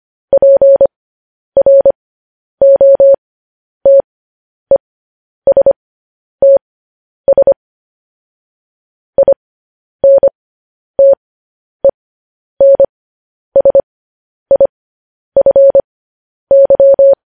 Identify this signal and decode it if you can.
Morse code, heard as a keyed tone, about 25 wpm, spaced as for 10 wpm, 550 Hz. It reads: PROTESTS INTENSIFY
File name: podcast / News headlines-1x-25x10